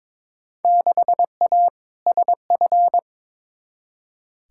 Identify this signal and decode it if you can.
Morse code, heard as a keyed tone, about 21 wpm, 700 Hz. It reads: 6A SF
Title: Field Day Exchanges - 22wpm